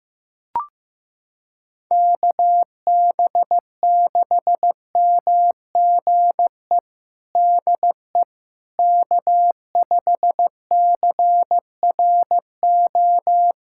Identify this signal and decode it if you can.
Morse code, heard as a keyed tone, about 15 wpm, 700 Hz. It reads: KB6MGE DE K5CRO
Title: USA Callsign DE Callsign 15wpm